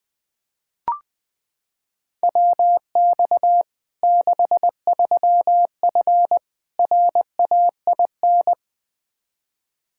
Morse code, heard as a keyed tone, about 20 wpm, 700 Hz. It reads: WX 63F RAIN
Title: Practice QSO - Exchange 2 - Info - 20wpm